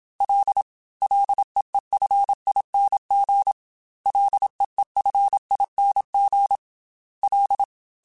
Morse code, heard as a keyed tone, about 27 wpm, 800 Hz. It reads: L LEEFING LEEFING L
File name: F and L / F and L mp3 / F and L Prefixes 3x2 Twice @25